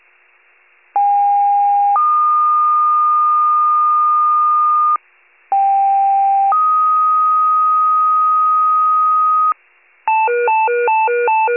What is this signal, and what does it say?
Two-tone page, heard as a one-second tone, then a three-second tone, about 810 then 1220 Hz, about 790 then 1250 Hz. It is heard from 0.9 to 9.5 s.
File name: wcalert